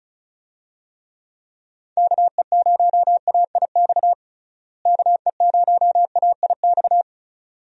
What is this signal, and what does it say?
Morse code, heard as a keyed tone, about 36 wpm, 700 Hz. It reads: KE0AIX KE0AIX
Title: USA Call Signs - 6 Characters - Repeated 2x - 35wpm